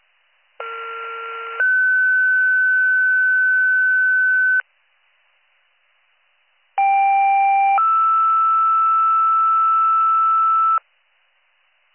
Two-tone page, heard as a one-second tone, then a three-second tone, about 1370 then 1500 Hz, about 790 then 1320 Hz. It is heard from 0.6 to 10.8 s.